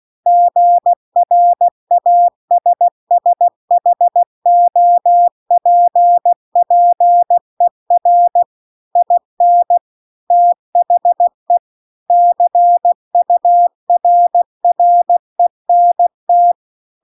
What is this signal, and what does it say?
Morse code, heard as a keyed tone, about 16 wpm, 700 Hz. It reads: GRASSHOPPER IN THE CURRENT
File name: CW-The-Big-Twoheaded-River-2-16WPM